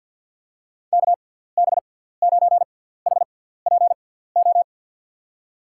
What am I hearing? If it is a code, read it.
Morse code, heard as a keyed tone, about 52 wpm, spaced as for 17 wpm, 700 Hz. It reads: KB9HPO
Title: USA Call Signs - 6 Characters - 6x Character Spacing - 50wpm